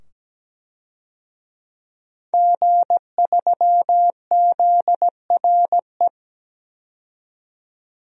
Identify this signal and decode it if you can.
Morse code, heard as a keyed tone, about 17 wpm, 700 Hz. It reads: G3ZRE
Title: DXCC-100 Call Signs - 5 Characters - 17wpm